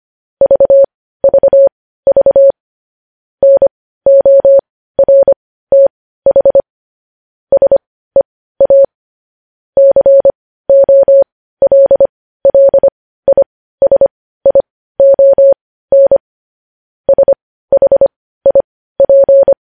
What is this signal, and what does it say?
Morse code, heard as a keyed tone, about 25 wpm, spaced as for 15 wpm, 550 Hz. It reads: VVV NORTH SEA COLLISION SHIP
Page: news headlines in morse code at 15 wpm